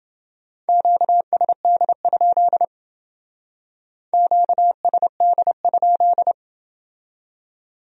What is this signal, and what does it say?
Morse code, heard as a keyed tone, about 29 wpm, 700 Hz. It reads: QSD? QSD?